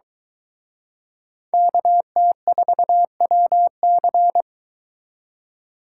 Morse code, heard as a keyed tone, about 24 wpm, 700 Hz. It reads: KT4WC